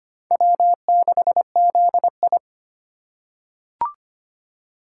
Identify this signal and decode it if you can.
Morse code, heard as a keyed tone, about 25 wpm, 700 Hz. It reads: W6ZI
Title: USA Call Signs - 4 Characters - 3x Character Spacing - 25wpm